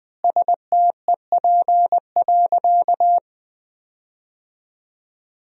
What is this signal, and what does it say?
Morse code, heard as a keyed tone, about 20 wpm, 700 Hz. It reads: STEP.
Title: 3x Word-Spacing - Sentences from Top 400 Words - 20wpm